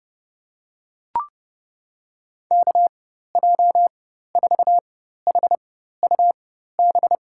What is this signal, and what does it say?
Morse code, heard as a keyed tone, about 30 wpm, spaced as for 14 wpm, 700 Hz. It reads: KJ4HUB
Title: USA Call Signs - 4x Character Spacing - 30wpm